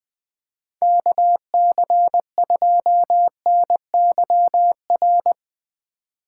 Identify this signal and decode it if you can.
Morse code, heard as a keyed tone, about 20 wpm, 700 Hz. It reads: KC2NYR
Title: USA Call Signs - 6 Characters - 4x Character Spacing - 20wpm